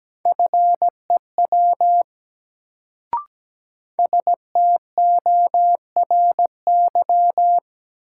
Morse code, heard as a keyed tone, about 17 wpm, 700 Hz. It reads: FEW STORY
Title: Sentences from Top 300 Words - New Word Drill - 17wpm